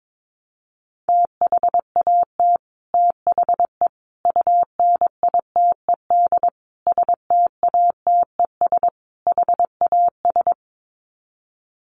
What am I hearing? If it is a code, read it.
Morse code, heard as a keyed tone, about 22 wpm, 700 Hz. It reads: THAT THE UNITED STATES HAS
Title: Sets of 5 Words Encore 22wpm